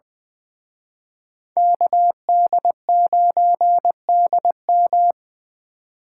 Morse code, heard as a keyed tone, about 23 wpm, 700 Hz. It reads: KD9DM
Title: Speed Racing - USA 2x2 Call Signs 20wpm